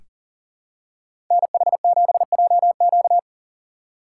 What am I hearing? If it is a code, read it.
Morse code, heard as a keyed tone, about 40 wpm, 700 Hz. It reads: NH7JQ
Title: USA Call Signs - 5 Characters - 6x Character Spacing - 40wpm